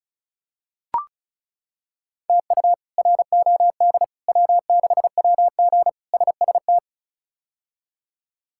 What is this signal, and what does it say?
Morse code, heard as a keyed tone, about 36 wpm, 700 Hz. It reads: TU ROD W6WG SST
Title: K1USN SST QSO Practice - 35wpm